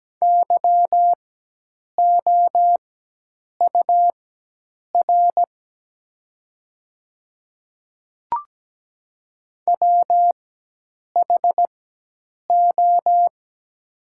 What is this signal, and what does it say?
Morse code, heard as a keyed tone, about 17 wpm, spaced as for 8 wpm, 700 Hz. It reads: YOUR WHO
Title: 4x Character Spacing - Top 100 Words - 17wpm